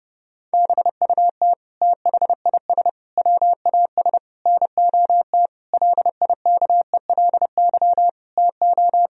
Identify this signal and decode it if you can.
Morse code, heard as a keyed tone, about 32 wpm, 700 Hz. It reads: BUT THIS WAS NOT LIKELY TO
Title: Sentences from Top 1000 Words - Review Part 4 - 30wpm